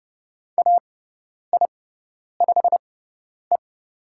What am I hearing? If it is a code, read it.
Morse code, heard as a keyed tone, about 30 wpm, spaced as for 10 wpm, 700 Hz. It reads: AI5E
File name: Lesson 16g - Letter c - Call Signs 30wpm